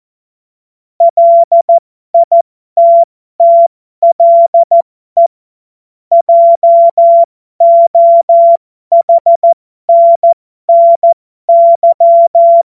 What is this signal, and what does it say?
Morse code, heard as a keyed tone, about 14 wpm, spaced as for 12 wpm, 680 Hz. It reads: LITTLE JOHNNY
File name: Morsekurs / Witze